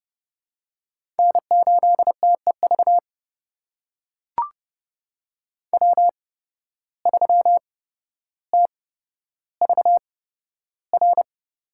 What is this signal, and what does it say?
Morse code, heard as a keyed tone, about 30 wpm, spaced as for 13 wpm, 700 Hz. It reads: N8TEV W3TVR
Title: USA Call Signs - 5 Characters - 8x Character Spacing - 30wpm